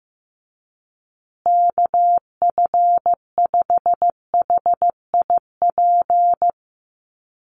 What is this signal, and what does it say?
Morse code, heard as a keyed tone, about 15 wpm, 700 Hz. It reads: KF5HIP
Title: USA Call Signs Repeated 3x 15wpm